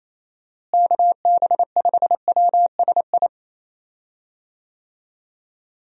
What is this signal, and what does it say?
Morse code, heard as a keyed tone, about 28 wpm, 700 Hz. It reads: KB5WSI
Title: USA Call Signs - 6 Characters - Repeated 3x - 28wpm